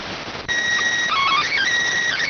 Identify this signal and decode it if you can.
SSTV (slow-scan television), PD 290.